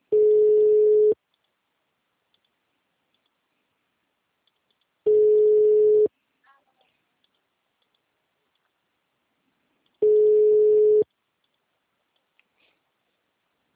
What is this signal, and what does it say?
Ringback tone (Europe, 425 Hz).
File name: out- -2002- - - .85622